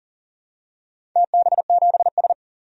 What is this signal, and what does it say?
Morse code, heard as a keyed tone, about 38 wpm, 690 Hz. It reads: TB7S
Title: DXCC-100 Call Signs - 4 Characters - 40wpm